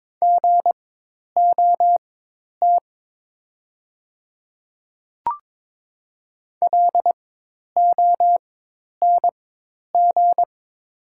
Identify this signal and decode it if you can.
Morse code, heard as a keyed tone, about 22 wpm, spaced as for 10 wpm, 700 Hz. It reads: GOT LONG
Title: 4x Character Spacing - Top 300 Words - 22wpm